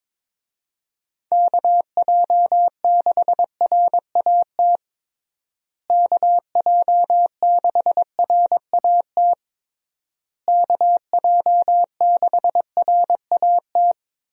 Morse code, heard as a keyed tone, about 22 wpm, 700 Hz. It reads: KJ6RAT KJ6RAT KJ6RAT
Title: USA Call Signs - 6 Characters - Repeated 3x - 22wpm